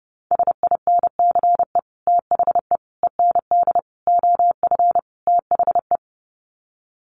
Morse code, heard as a keyed tone, about 30 wpm, 700 Hz. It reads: SINCE THE END OF THE
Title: Sets of 5 Words Encore 30wpm